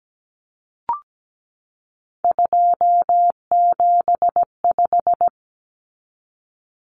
Morse code, heard as a keyed tone, about 17 wpm, 700 Hz. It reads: 275
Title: Common Cut Numbers - 3-Digit 17wpm